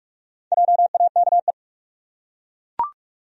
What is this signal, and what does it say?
Morse code, heard as a keyed tone, about 42 wpm, 700 Hz. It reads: JAKE
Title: USA Common Names - Max 4 Letters - 45wpm